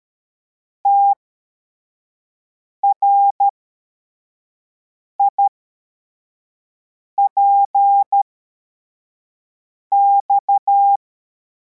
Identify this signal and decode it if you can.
Morse code, heard as a keyed tone, about 13 wpm, spaced as for 4 wpm, 800 Hz. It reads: TRIPX